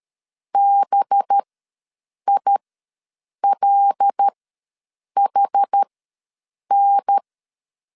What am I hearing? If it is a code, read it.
Morse code, heard as a keyed tone, about 13 wpm, spaced as for 7 wpm, 800 Hz. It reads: BILHN
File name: 5 LB ws 12wpm